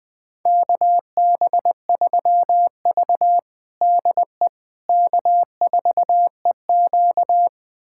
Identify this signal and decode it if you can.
Morse code, heard as a keyed tone, about 20 wpm, 700 Hz. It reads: KB3V DE K4EQ
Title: USA Callsign DE Callsign - 4 Characters - 20wpm